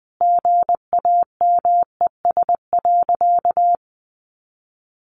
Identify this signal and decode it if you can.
Morse code, heard as a keyed tone, about 20 wpm, 700 Hz. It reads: GAMES.
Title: Additional Easy Phrases 20wpm